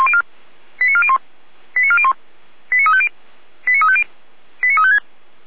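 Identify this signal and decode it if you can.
Five-tone selective call (ZVEI), heard as quick runs of five tones, tones of about 70 ms. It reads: 89401 89401 89350 89350 89356
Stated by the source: ZVEI five-tone call received by SDR.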